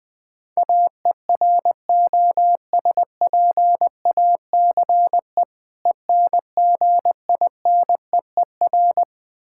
Morse code, heard as a keyed tone, about 20 wpm, 700 Hz. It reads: AEROSPACE ENGINEER
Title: K1USN SST QSO Practice 20wpm